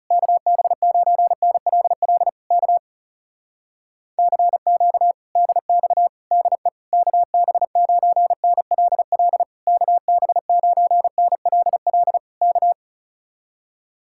Morse code, heard as a keyed tone, about 38 wpm, 700 Hz. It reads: KB9NLL K CQ DX DE KB9NLL KB9NLL K